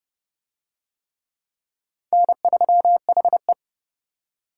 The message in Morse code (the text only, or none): N3HE